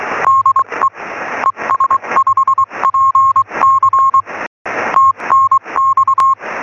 Morse code, heard as a keyed tone, about 23 wpm, 1070 Hz, with radio noise. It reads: DE ES5PC TNX
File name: sm7lcb / 0611 es5pc 23cm